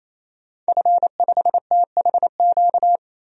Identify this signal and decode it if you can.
Morse code, heard as a keyed tone, about 28 wpm, 700 Hz. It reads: F5THQ
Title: DXCC-100 Call Signs - 5 Characters - 28wpm